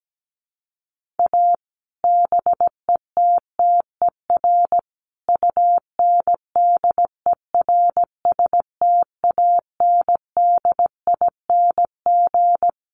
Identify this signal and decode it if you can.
Morse code, heard as a keyed tone, about 17 wpm, 700 Hz. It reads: A BETTER UNDERSTANDING
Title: Sets of 5 Words Encore2 17wpm